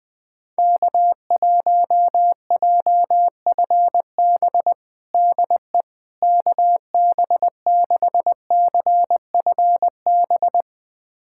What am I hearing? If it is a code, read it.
Morse code, heard as a keyed tone, about 21 wpm, 700 Hz. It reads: K1JFB DE KB6CFB
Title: USA Callsign DE Callsign 20wpm